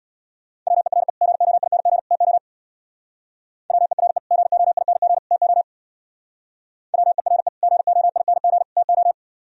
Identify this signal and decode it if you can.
Morse code, heard as a keyed tone, about 62 wpm, 700 Hz. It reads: WERE GOING TO WERE GOING TO WERE GOING TO
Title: Speed-Racing - Sets of 3 Words 50wpm